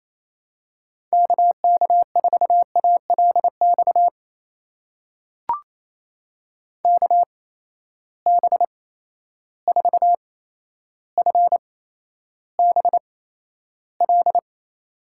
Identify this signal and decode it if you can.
Morse code, heard as a keyed tone, about 28 wpm, spaced as for 12 wpm, 700 Hz. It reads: KK4ALX KB4FBL
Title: USA Call Signs - 6 Characters - 8x Character Spacing - 28wpm